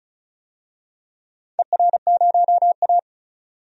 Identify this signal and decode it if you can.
Morse code, heard as a keyed tone, about 36 wpm, 700 Hz. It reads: ER0A
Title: DXCC-100 Call Signs - 4 Characters - 35wpm